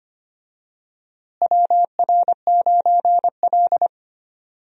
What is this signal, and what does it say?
Morse code, heard as a keyed tone, about 25 wpm, 700 Hz. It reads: WR9L
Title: USA Call Signs - 4 Characters - Repeated 3x - 25wpm